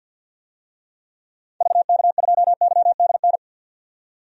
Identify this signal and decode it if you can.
Morse code, heard as a keyed tone, about 48 wpm, 700 Hz. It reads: VK2YDN
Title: DXCC-100 Call Signs - 6 Characters - 50wpm